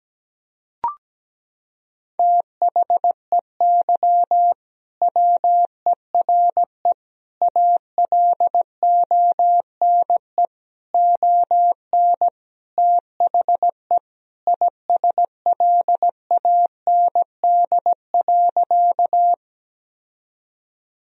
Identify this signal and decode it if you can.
Morse code, heard as a keyed tone, about 17 wpm, 700 Hz. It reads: THEY WERE ALONE ON THE ISLAND.
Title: Sentences from Top 1000 Words - Review Part 3 - 17wpm